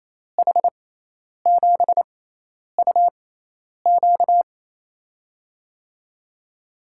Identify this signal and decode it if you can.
Morse code, heard as a keyed tone, about 27 wpm, spaced as for 10 wpm, 700 Hz. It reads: H7UQ
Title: USA Call Signs - 6x Character Spacing - 28wpm